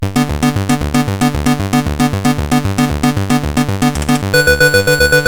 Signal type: none